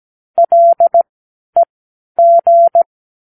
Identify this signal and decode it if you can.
Morse code, heard as a keyed tone, about 17 wpm, spaced as for 11 wpm, 680 Hz. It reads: LEG